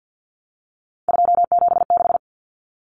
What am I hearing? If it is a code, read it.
Morse code, heard as a keyed tone, about 50 wpm, 700 Hz. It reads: 276
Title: Common Cut Numbers - 3-Digit 50wpm